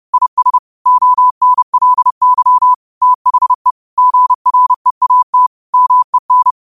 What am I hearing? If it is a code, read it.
Morse code, heard as a keyed tone, about 30 wpm, 1000 Hz. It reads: IS ONLY THE GREAT MEN
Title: 30 WPM morse code quotes for Fri, 01 Mar 2024